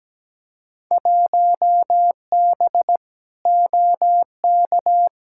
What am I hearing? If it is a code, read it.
Morse code, heard as a keyed tone, about 17 wpm, 700 Hz. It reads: 1B OK
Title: Field Day Exchanges Encore - 17wpm